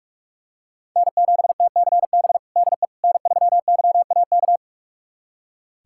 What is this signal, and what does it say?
Morse code, heard as a keyed tone, about 46 wpm, 700 Hz. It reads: N7TCB DE N3YAK
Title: USA Callsign DE Callsign - 5 Characters - 45wpm